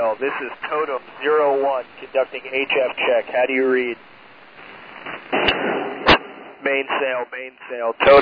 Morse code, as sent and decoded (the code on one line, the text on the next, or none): none